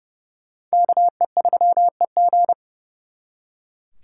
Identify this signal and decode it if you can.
Morse code, heard as a keyed tone, about 36 wpm, 700 Hz. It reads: KE3EG
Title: Speed Racing - USA 2x2 Call Signs 30wpm